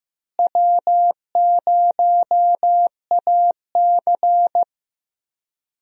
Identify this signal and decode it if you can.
Morse code, heard as a keyed tone, about 15 wpm, 700 Hz. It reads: W0AC
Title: USA Call Signs - 4 Characters - Repeated 2x - 15wpm